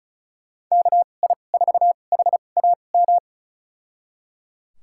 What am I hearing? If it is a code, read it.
Morse code, heard as a keyed tone, about 34 wpm, spaced as for 25 wpm, 700 Hz. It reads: KI4HAM
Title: USA Call Signs - 6 Characters - 2x Character Spacing - 35wpm